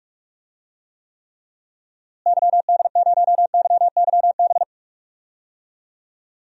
Morse code, heard as a keyed tone, about 44 wpm, 700 Hz. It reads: YD0YYB
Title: DXCC-100 Call Signs - 6 Characters - Repeated 3x - 45wpm